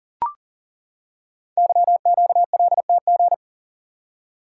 Morse code, heard as a keyed tone, about 38 wpm, 700 Hz. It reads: YQLTG
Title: Code Groups - 5 Letters - 40wpm